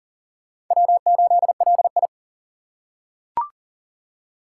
Morse code, heard as a keyed tone, about 40 wpm, 700 Hz. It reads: W8LI
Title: USA Call Signs - 4 Characters - 6x Character Spacing - 40wpm